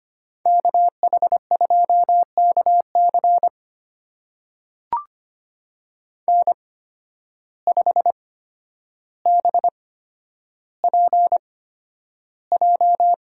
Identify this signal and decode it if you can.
Morse code, heard as a keyed tone, about 25 wpm, spaced as for 11 wpm, 700 Hz. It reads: KH2KC N5BPJ
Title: USA Call Signs - 5 Characters - 8x Character Spacing - 25wpm